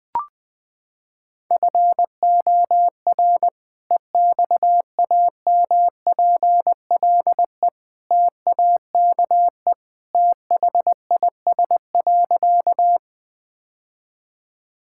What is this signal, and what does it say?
Morse code, heard as a keyed tone, about 20 wpm, 700 Hz. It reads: FOR EXAMPLE TAKE THIS.
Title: Sentences with 2-Word Phrases - 20wpm